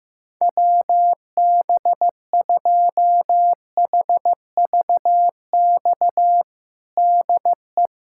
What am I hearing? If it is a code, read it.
Morse code, heard as a keyed tone, about 15 wpm, 700 Hz. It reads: WB2HVX DE
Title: USA Callsign DE Callsign - 6 Characters - 15wpm